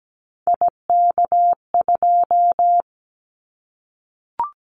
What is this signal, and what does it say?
Morse code, heard as a keyed tone, about 17 wpm, 700 Hz. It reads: IK2
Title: Callsign Prefixes for Top 25 DXCC - Repeated 3x - 17wpm